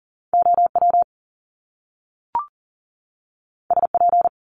Morse code, heard as a keyed tone, about 38 wpm, 700 Hz. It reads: OW SP
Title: Top Two Letter Combinations in Words 40wpm